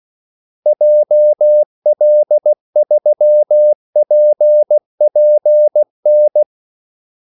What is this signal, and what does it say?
Morse code, heard as a keyed tone, about 16 wpm, 600 Hz. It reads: JL3PPN
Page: Kaldesignaler 16 wpm